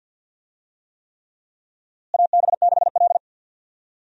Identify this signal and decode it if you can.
Morse code, heard as a keyed tone, about 51 wpm, 700 Hz. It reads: AB6L
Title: USA Call Signs - 4 Characters - Repeated 3x - 50wpm